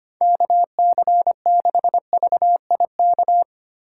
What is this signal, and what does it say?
Morse code, heard as a keyed tone, about 31 wpm, 700 Hz. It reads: KC6VIK